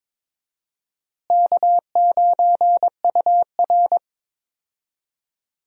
Morse code, heard as a keyed tone, about 21 wpm, 700 Hz. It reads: K9UR
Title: USA Call Signs - 4 Characters - Repeated 2x - 22wpm